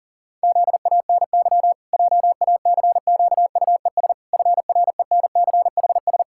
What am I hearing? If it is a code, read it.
Morse code, heard as a keyed tone, about 40 wpm, 700 Hz. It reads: ZANY JACQUES FRENCHS
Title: Pangram Challenge v2 - 40wpm